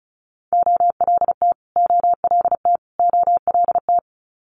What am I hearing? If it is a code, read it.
Morse code, heard as a keyed tone, about 36 wpm, 700 Hz. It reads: OLT OLT OLT
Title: Multi-letter Phonemes Repeated 3x 35wpm